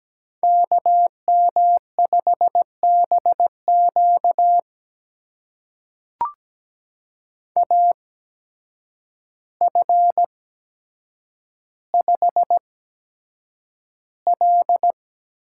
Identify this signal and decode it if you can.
Morse code, heard as a keyed tone, about 17 wpm, spaced as for 8 wpm, 700 Hz. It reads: KM5BQ AF5L
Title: USA Call Signs - 5 Characters - 8x Character Spacing - 17wpm